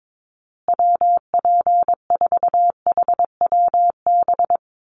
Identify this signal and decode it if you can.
Morse code, heard as a keyed tone, about 22 wpm, 700 Hz. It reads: WP4HWB
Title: USA Call Signs Repeated 2x 22wpm